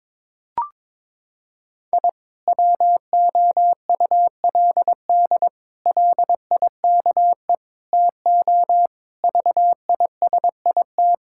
Morse code, heard as a keyed tone, about 22 wpm, 700 Hz. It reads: I WOULD LIKE TO VISIT
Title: Sentences from Top 1000 Words - Review Part 6 - 22wpm